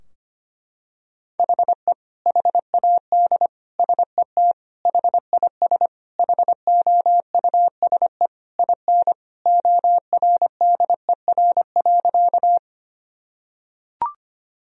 Morse code, heard as a keyed tone, about 25 wpm, 700 Hz. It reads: HE HAD SET HIS HOUSE IN ORDER.
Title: Sentences from Top 1000 Words - Review Part 2 - 25wpm